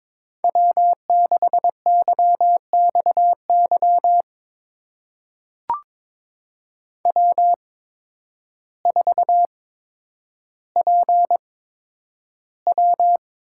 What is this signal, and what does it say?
Morse code, heard as a keyed tone, about 22 wpm, spaced as for 10 wpm, 700 Hz. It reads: W6YXY W4PW
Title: USA Call Signs - 5 Characters - 8x Character Spacing - 22wpm